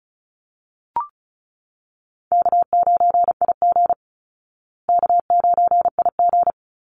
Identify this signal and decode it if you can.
Morse code, heard as a keyed tone, about 34 wpm, 700 Hz. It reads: K9IG K9IG
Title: Easy Call Signs Repeated 3x - 35wpm